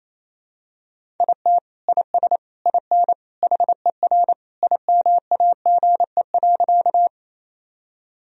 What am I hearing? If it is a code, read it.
Morse code, heard as a keyed tone, about 28 wpm, 700 Hz. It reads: IT IS IN HER IMAGE.
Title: Sentences from Top 1000 Words - Review Part 4 - 28wpm